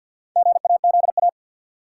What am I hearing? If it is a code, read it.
Morse code, heard as a keyed tone, about 53 wpm, 700 Hz. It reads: GAZA
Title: Sets of 5 Words - New Word Drill - 50wpm